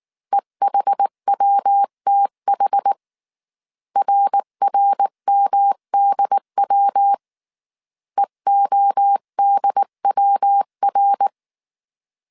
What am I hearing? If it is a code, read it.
Morse code, heard as a keyed tone, about 19 wpm, 800 Hz. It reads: EHWTH RRMDW EODWR
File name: CW / DET / 1 3 1500C S B 18wpm 1916